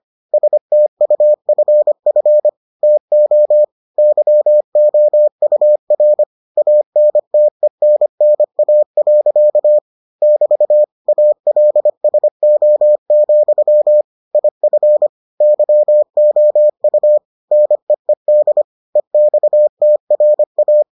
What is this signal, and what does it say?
Morse code, heard as a keyed tone, about 25 wpm, 600 Hz. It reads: STUFF TO YOUR ANTENNA. <BT> ALSO, IF YOU NEED EXTRA